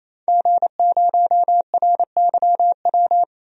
Morse code, heard as a keyed tone, about 28 wpm, 700 Hz. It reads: G0RYW
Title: DXCC-100 Call Signs - 5 Characters - 28wpm